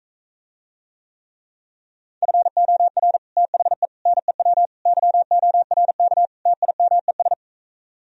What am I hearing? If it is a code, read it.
Morse code, heard as a keyed tone, about 45 wpm, 700 Hz. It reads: FOR THE NEW YORK TIMES